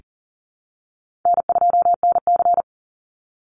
Morse code, heard as a keyed tone, about 40 wpm, 700 Hz. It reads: N2NC
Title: Call Signs Repeated 3x - 40wpm